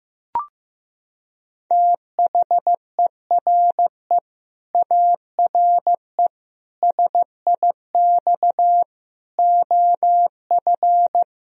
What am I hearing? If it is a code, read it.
Morse code, heard as a keyed tone, about 15 wpm, 700 Hz. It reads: THERE ARE SIX OF
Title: Sentences from Top 1000 Words - Review Part 2 - 15wpm